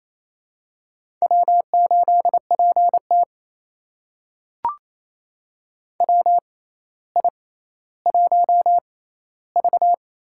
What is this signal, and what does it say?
Morse code, heard as a keyed tone, about 29 wpm, spaced as for 15 wpm, 700 Hz. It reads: W8PT WI1V
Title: USA Call Signs - 4 Characters - 6x Character Spacing - 28wpm